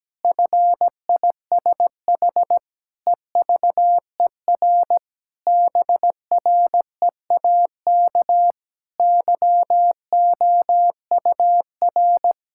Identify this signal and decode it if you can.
Morse code, heard as a keyed tone, about 17 wpm, 700 Hz. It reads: FISH EVER BREAK YOUR
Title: Sentences from Top 1000 Words - Review Part 6 - 17wpm